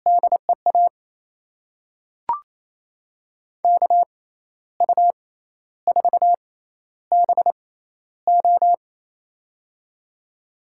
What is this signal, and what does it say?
Morse code, heard as a keyed tone, about 27 wpm, spaced as for 12 wpm, 700 Hz. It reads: DEA KU4BO
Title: USA Call Signs - 5 Characters - 6x Character Spacing - 28wpm